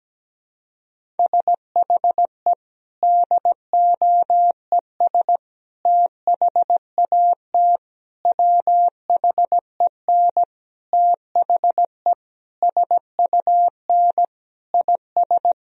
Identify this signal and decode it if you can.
Morse code, heard as a keyed tone, about 17 wpm, 700 Hz. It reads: SHE DOES THAT WHEN THE SUN IS
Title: Sentences from Top 1000 Words - Review Part 1 - 17wpm